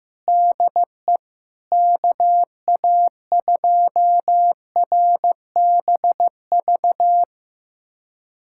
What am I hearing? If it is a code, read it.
Morse code, heard as a keyed tone, about 15 wpm, 700 Hz. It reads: DE KA2RBV
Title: USA Callsign DE Callsign - 6 Characters - 15wpm